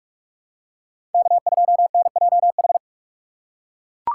Morse code, heard as a keyed tone, about 41 wpm, 700 Hz. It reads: K2NJH
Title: USA Call Signs - 5 Characters - 2x Character Spacing - 45wpm